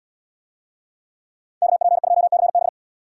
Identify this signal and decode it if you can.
Morse code, heard as a keyed tone, about 77 wpm, 700 Hz. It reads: KP2CB